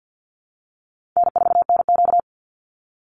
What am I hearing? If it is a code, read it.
Morse code, heard as a keyed tone, about 50 wpm, 700 Hz. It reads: N4NQ
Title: Call Signs Repeated 2x - 50wpm